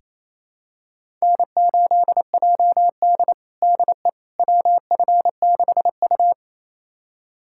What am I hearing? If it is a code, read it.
Morse code, heard as a keyed tone, about 28 wpm, 700 Hz. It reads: N8JD DE WF6U
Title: USA Callsign DE Callsign - 4 Characters - 28wpm